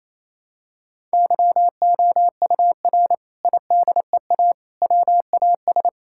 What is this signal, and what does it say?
Morse code, heard as a keyed tone, about 28 wpm, 700 Hz. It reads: YOUR IDEA WAS
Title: Sentences from Top 1000 Words - Review Part 6 - 28wpm